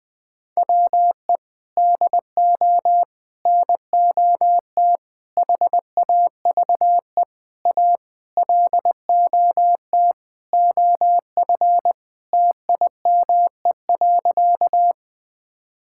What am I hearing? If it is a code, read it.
Morse code, heard as a keyed tone, about 20 wpm, 700 Hz. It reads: WE DO NOT HAVE A LOT OF TIME.
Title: Sentences with 5-Word Phrases - Spoken Beginning - 20wpm